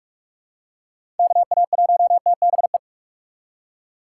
Morse code, heard as a keyed tone, about 46 wpm, 700 Hz. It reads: KA1TBE
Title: USA Call Signs - 6 Characters - 3x Character Spacing - 45wpm